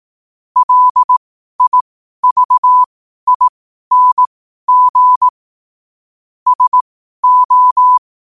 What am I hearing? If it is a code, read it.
Morse code, heard as a keyed tone, about 18 wpm, spaced as for 12 wpm, 1000 Hz. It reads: LIVING SO
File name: qotdfiles / 12wpm